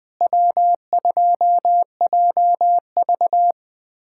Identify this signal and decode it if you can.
Morse code, heard as a keyed tone, about 20 wpm, 700 Hz. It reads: W2JV
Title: USA Callsign DE Callsign - 4 Characters - 20wpm